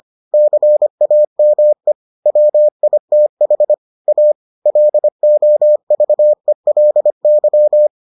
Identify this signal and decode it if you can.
Morse code, heard as a keyed tone, about 25 wpm, 600 Hz. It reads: CAME WITH A LOVELY